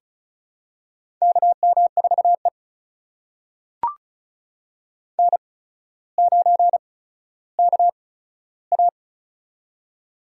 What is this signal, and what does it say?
Morse code, heard as a keyed tone, about 35 wpm, spaced as for 15 wpm, 700 Hz. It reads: KM4E N9KA
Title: USA Call Signs - 4 Characters - 8x Character Spacing - 35wpm